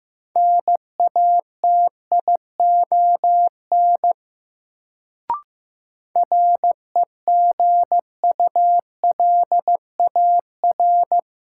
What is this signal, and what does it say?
Morse code, heard as a keyed tone, about 15 wpm, 700 Hz. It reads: NATION REGULAR
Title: Sentences from Top 900 Words - New Word Drill - 15wpm